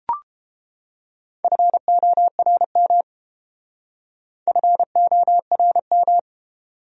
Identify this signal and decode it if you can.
Morse code, heard as a keyed tone, about 28 wpm, 700 Hz. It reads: FORM FORM